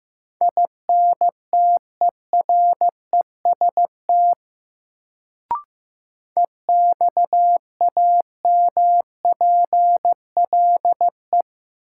Morse code, heard as a keyed tone, about 15 wpm, 700 Hz. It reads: INTEREST EXAMPLE
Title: Sentences from Top 400 Words - New Word Drill - 15wpm